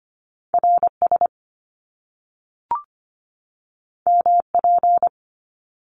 Morse code, heard as a keyed tone, about 25 wpm, 700 Hz. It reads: RS MP
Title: Top Two Letter Combinations in Words 25wpm